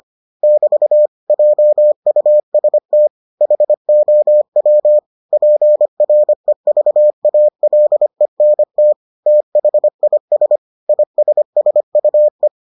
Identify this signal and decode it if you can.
Morse code, heard as a keyed tone, about 25 wpm, 600 Hz. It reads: <BT> JUST HOW PREVALENT THIS ISSUE